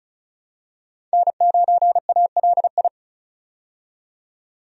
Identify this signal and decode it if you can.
Morse code, heard as a keyed tone, about 36 wpm, 700 Hz. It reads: N9ALI
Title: USA Call Signs - 5 Characters - 35wpm